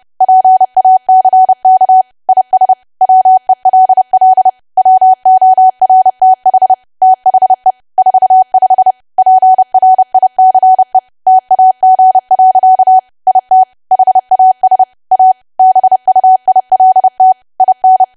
Morse code, heard as a keyed tone, about 30 wpm, 750 Hz. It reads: PACK IS WELL WORTH THE 45 PRICE TAG. IT HAS A BUILT IN
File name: cwa / W1AW-Level3 / 30WPM